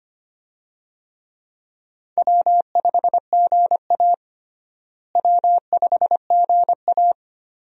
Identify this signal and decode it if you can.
Morse code, heard as a keyed tone, about 25 wpm, 700 Hz. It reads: W5GA W5GA
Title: USA Call Signs - 4 Characters - Repeated 2x - 25wpm